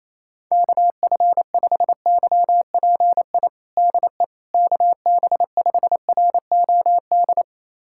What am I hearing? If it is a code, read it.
Morse code, heard as a keyed tone, about 28 wpm, 700 Hz. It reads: KF5YPI DE KB5ROD
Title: USA Callsign DE Callsign 28wpm